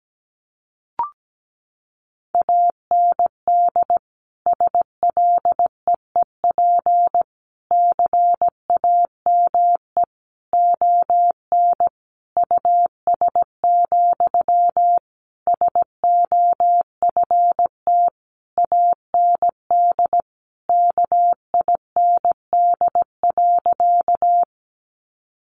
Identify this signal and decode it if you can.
Morse code, heard as a keyed tone, about 17 wpm, 700 Hz. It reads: AND SLEEP CAME ON US, SOFT AND KIND.
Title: Sentences from Top 1500 Words 17wpm